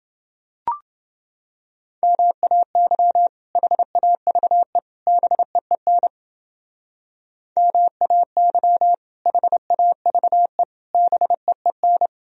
Morse code, heard as a keyed tone, about 29 wpm, 700 Hz. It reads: MAY HAVE BEEN MAY HAVE BEEN